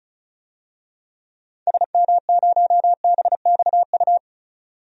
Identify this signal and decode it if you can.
Morse code, heard as a keyed tone, about 34 wpm, 700 Hz. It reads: SM0BXU
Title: DXCC-100 Call Signs - 6 Characters - 35wpm